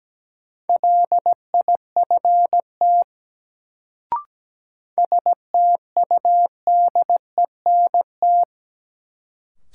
Morse code, heard as a keyed tone, about 17 wpm, 700 Hz. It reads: LIFT STUDENT
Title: Sentences from Top 1000 Words - New Word Drill - 17wpm